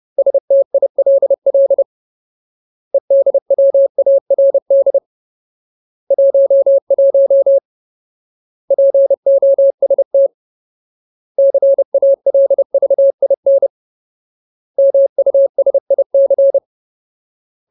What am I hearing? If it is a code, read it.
Morse code, heard as a keyed tone, about 30 wpm, 550 Hz. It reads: STILL EDWARD 11 POST CALVIN MUSIC
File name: Ultimate Word List / 30WPM Ultimate